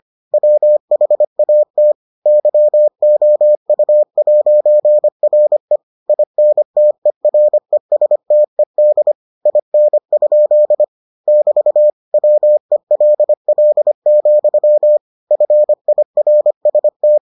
Morse code, heard as a keyed tone, about 25 wpm, 600 Hz. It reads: WHAT YOU'RE INTERESTED IN? <BT> WELL, FIRST